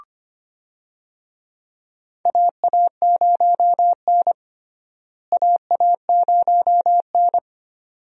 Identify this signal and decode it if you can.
Morse code, heard as a keyed tone, about 25 wpm, 700 Hz. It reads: AA0N AA0N
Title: USA Call Signs - 4 Characters - Repeated 2x - 25wpm